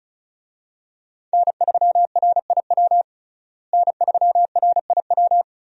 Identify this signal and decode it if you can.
Morse code, heard as a keyed tone, about 34 wpm, 700 Hz. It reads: N3RIW N3RIW
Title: USA Call Signs - 5 Characters - Repeated 3x - 35wpm